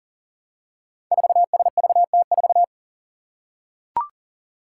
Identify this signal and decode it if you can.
Morse code, heard as a keyed tone, about 40 wpm, 700 Hz. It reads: 4SVT4
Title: Five Challenging Letter-Number Combinations 40wpm